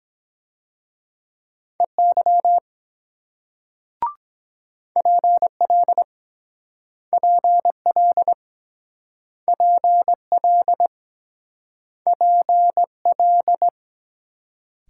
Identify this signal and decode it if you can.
Morse code, heard as a keyed tone, about 23 wpm, 700 Hz. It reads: EY PL PL PL PL